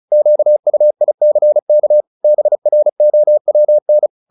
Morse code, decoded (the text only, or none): QUICK BROWN